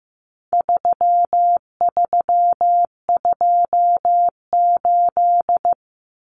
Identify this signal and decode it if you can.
Morse code, heard as a keyed tone, about 15 wpm, 700 Hz. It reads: 3328
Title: Common Cut Numbers - 4-Digit 15wpm